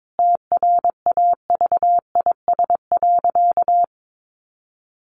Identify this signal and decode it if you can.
Morse code, heard as a keyed tone, about 22 wpm, 700 Hz. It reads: TRAVIS.
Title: A Sound Of Thunder 22wpm